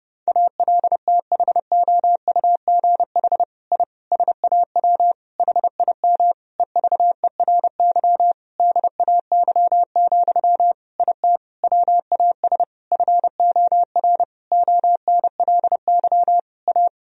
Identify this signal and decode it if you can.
Morse code, heard as a keyed tone, about 30 wpm, 700 Hz. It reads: ALTHOUGH I SAW HIM EVERY DAY, IT WAS FOR ONLY A
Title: Sentences from Top 1000 Words - Review Part 3 - 30wpm